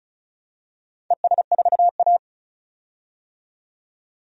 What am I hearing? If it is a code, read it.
Morse code, heard as a keyed tone, about 34 wpm, 700 Hz. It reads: ES4A
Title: DXCC-100 Call Signs - 4 Characters - 35wpm